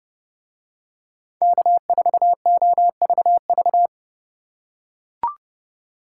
Morse code, heard as a keyed tone, about 29 wpm, 700 Hz. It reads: K4OVV